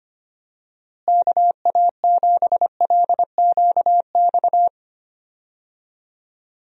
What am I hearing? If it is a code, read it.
Morse code, heard as a keyed tone, about 25 wpm, 700 Hz. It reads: KA7LQX
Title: USA Call Signs - 6 Characters - Repeated 2x - 25wpm